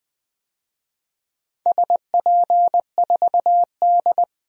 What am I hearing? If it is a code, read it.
Morse code, heard as a keyed tone, about 20 wpm, 700 Hz. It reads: SP4D
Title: DXCC-100 Call Signs - 4 Characters - 20wpm